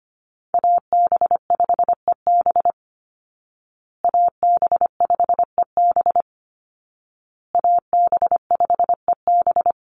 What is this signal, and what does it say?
Morse code, heard as a keyed tone, about 25 wpm, 700 Hz. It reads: AB5EB AB5EB AB5EB
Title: Call Signs Repeated 2x - 25wpm